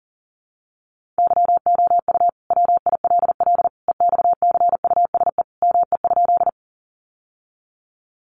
Morse code, heard as a keyed tone, about 40 wpm, 700 Hz. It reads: YOU WILL EXCUSE ME?
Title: Sentences from Top 700 Words 40wpm